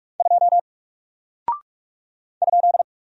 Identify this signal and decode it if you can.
Morse code, heard as a keyed tone, about 46 wpm, 700 Hz. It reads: 2 ?
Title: Single Character (Letter:Number:Punctuation:Prosign) Alt 45wpm